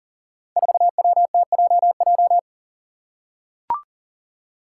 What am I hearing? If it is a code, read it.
Morse code, heard as a keyed tone, about 41 wpm, 700 Hz. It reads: 4WTJJ
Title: Five Challenging Letter-Number Combinations 40wpm